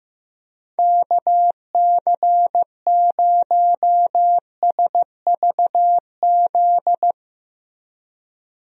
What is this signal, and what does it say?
Morse code, heard as a keyed tone, about 15 wpm, 700 Hz. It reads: KC0SVZ
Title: USA Call Signs - 6 Characters - Repeated 3x - 15wpm